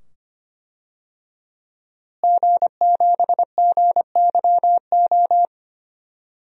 Morse code, heard as a keyed tone, about 25 wpm, 700 Hz. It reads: G7GYO
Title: DXCC-100 Call Signs - 5 Characters - Repeated 2x - 25wpm